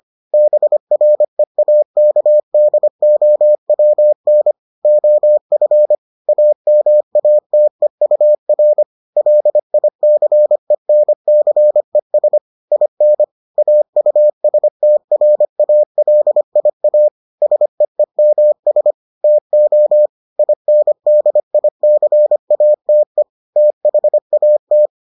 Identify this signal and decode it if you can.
Morse code, heard as a keyed tone, about 25 wpm, 600 Hz. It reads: BREAKDOWN OF AMATEUR LICENCES IN AUSTRALIA SEEMS TO INDICATE THAT